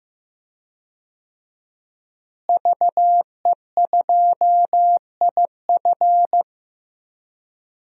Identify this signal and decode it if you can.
Morse code, heard as a keyed tone, about 15 wpm, 700 Hz. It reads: VE2IF
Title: DXCC-100 Call Signs - 5 Characters - Repeated 3x - 15wpm